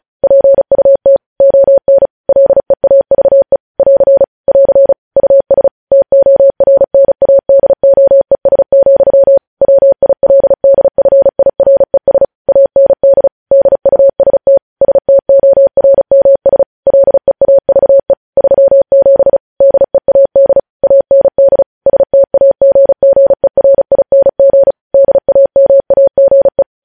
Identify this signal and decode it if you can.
Morse code, heard as a keyed tone, about 36 wpm, 550 Hz. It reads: PUT ON LEAVE <AR> <AR> US TORNADOES, WILDFIRES AND DUST STORMS LEAVE 37 DEAD AND STAGGERING DAMAGE
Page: News Headlines in Morse Code at 35 WPM